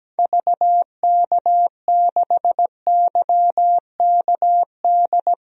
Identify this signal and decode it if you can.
Morse code, heard as a keyed tone, about 17 wpm, 700 Hz. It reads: VK6YKD
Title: DXCC-100 Call Signs - 6 Characters - Repeated 2x - 17wpm